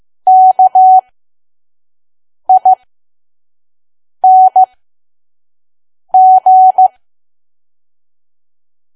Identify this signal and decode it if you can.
Morse code, heard as a keyed tone, about 15 wpm, spaced as for 5 wpm, 750 Hz. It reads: KING
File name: Archive / 5 WPM files / PM